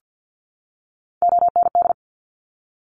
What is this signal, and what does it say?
Morse code, heard as a keyed tone, about 50 wpm, 700 Hz. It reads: OND
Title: Top Three Letter Combinations in Words 50wpm